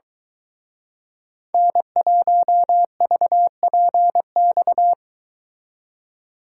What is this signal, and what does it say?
Morse code, heard as a keyed tone, about 23 wpm, 700 Hz. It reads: N1VPX